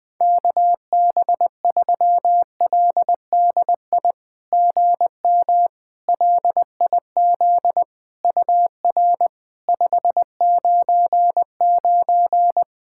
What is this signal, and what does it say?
Morse code, heard as a keyed tone, about 20 wpm, 700 Hz. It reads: KB3LDI GM LIZ UR 599
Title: SOTA QSO Practice 20wpm